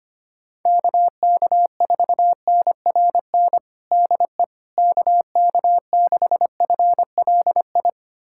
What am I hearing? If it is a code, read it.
Morse code, heard as a keyed tone, about 25 wpm, 700 Hz. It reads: KK4NRN DE KK6FLI
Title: USA Callsign DE Callsign - 6 Characters - 25wpm